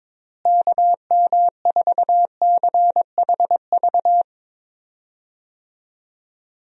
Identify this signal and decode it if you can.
Morse code, heard as a keyed tone, about 22 wpm, 700 Hz. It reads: KM4CHV
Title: USA Call Signs - 6 Characters - Repeated 2x - 22wpm